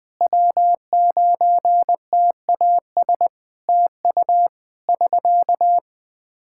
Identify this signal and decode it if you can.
Morse code, heard as a keyed tone, about 20 wpm, 700 Hz. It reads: W9TAS TU <SK>
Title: Practice QSO - Exchange 4 - Ending - 20wpm